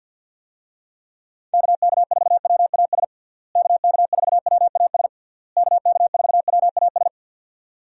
Morse code, heard as a keyed tone, about 47 wpm, 700 Hz. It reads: KK4WAS KK4WAS KK4WAS
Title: USA Call Signs - 6 Characters - Repeated 3x - 50wpm